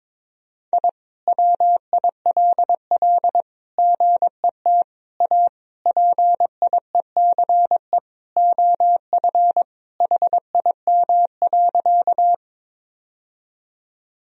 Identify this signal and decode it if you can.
Morse code, heard as a keyed tone, about 22 wpm, 700 Hz. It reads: I WILL GET A PIECE OF HIM.
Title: Sentences from Top 1000 Words - Review Part 2 - 22wpm